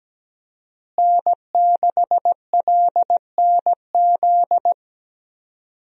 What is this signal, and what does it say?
Morse code, heard as a keyed tone, about 17 wpm, 700 Hz. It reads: N6LNZ
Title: USA Call Signs - 5 Characters - Repeated 2x - 17wpm